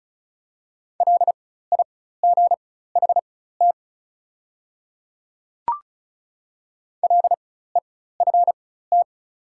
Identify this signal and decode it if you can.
Morse code, heard as a keyed tone, about 36 wpm, spaced as for 16 wpm, 690 Hz. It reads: LIGHT LEFT
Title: 4x Character Spacing - Top 300 Words - 35wpm